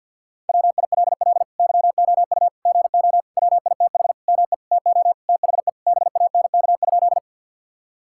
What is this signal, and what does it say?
Morse code, heard as a keyed tone, about 50 wpm, 700 Hz. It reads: WILL YOU GO WITH ME TO THE BANK?
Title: Sentences from Top 1000 Words - Review Part 6 - 50wpm